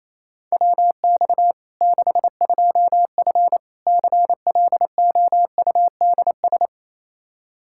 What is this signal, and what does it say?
Morse code, heard as a keyed tone, about 28 wpm, 700 Hz. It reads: WX 62F CLOUDS
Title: K1USN SST QSO Practice 28wpm